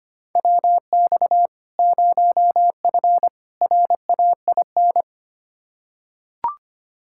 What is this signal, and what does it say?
Morse code, heard as a keyed tone, about 25 wpm, 700 Hz. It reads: WX 0F RAIN
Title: Practice QSO - Exchange 2 - Info - 25wpm